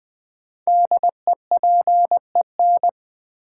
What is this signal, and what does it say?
Morse code, heard as a keyed tone, about 20 wpm, 700 Hz. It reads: DEPEN
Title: Build A Word - 5 Letters At A Time - 10-Letter Words - 20wpm